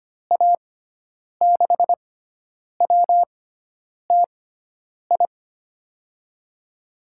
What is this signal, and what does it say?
Morse code, heard as a keyed tone, about 25 wpm, spaced as for 9 wpm, 700 Hz. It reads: A6WTI
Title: USA Call Signs - 6 Characters - 6x Character Spacing - 25wpm